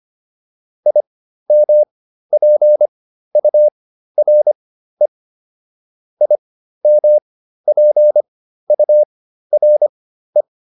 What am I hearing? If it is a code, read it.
Morse code, heard as a keyed tone, about 25 wpm, spaced as for 13 wpm, 600 Hz. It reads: IMPURE IMPURE